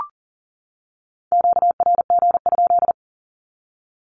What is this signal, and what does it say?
Morse code, heard as a keyed tone, about 40 wpm, 700 Hz. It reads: QRG?
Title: Q-Codes 40wpm